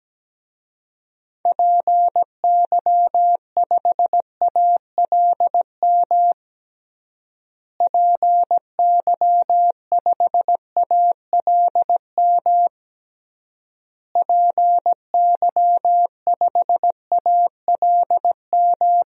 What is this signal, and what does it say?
Morse code, heard as a keyed tone, about 17 wpm, 700 Hz. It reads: PY5ALM PY5ALM PY5ALM
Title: DXCC-100 Call Signs - 6 Characters - Repeated 3x - 17wpm